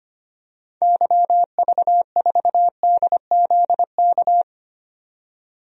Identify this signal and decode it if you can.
Morse code, heard as a keyed tone, about 25 wpm, 700 Hz. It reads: YV4DZK
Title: DXCC-100 Call Signs - 6 Characters - 25wpm